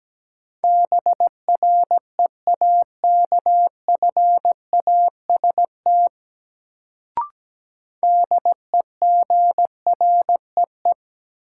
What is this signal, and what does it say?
Morse code, heard as a keyed tone, about 17 wpm, 700 Hz. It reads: BREAKFAST DEGREE
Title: Sentences from Top 1500 Words - New Word Drill - 17wpm